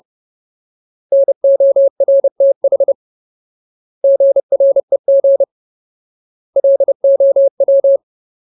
Morse code, heard as a keyed tone, about 30 wpm, 550 Hz. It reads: NORTH GREG LOW
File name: Ultimate Word List / 30WPM Ultimate